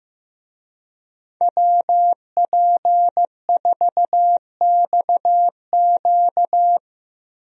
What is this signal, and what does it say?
Morse code, heard as a keyed tone, about 15 wpm, 700 Hz. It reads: WP4XQ
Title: USA Call Signs - 5 Characters - Repeated 2x - 15wpm